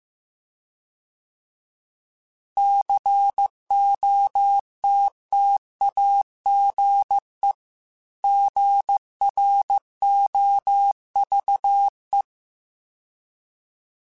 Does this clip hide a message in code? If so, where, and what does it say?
Yes: Morse code from 2.6 to 12.2 s, heard as a keyed tone, about 15 wpm, 800 Hz. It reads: COTTAGE GROVE